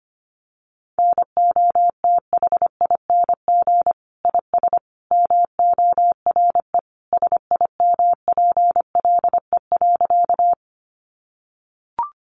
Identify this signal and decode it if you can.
Morse code, heard as a keyed tone, about 25 wpm, 700 Hz. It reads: NOTHING IS MORE SIMPLE.
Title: Sentences from Top 500 Words 25wpm